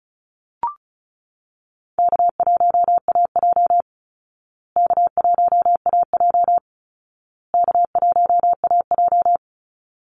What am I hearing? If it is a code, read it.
Morse code, heard as a keyed tone, about 34 wpm, 700 Hz. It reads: K1AJ K1AJ K1AJ
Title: Call Signs Repeated 2x - 35wpm